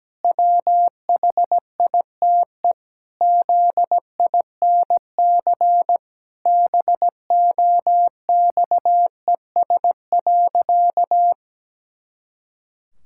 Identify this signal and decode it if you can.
Morse code, heard as a keyed tone, about 17 wpm, 700 Hz. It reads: WHITE ZINC BOXES.
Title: Pangram Challenge v2 - 17wpm